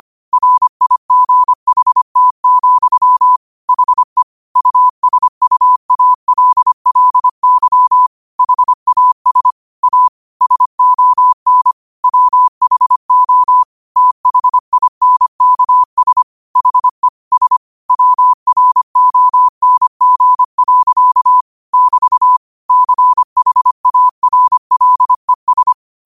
Morse code, heard as a keyed tone, about 25 wpm, 1000 Hz. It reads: RIGHT, HE USUALLY HAS A SON WHO THINKS HE S WRONG. <BT> CHARLES
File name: qotdfiles / 25wpm